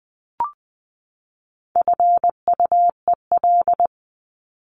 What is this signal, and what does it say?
Morse code, heard as a keyed tone, about 20 wpm, 700 Hz. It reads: FUEL
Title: Top 1751-2000 Words 20wpm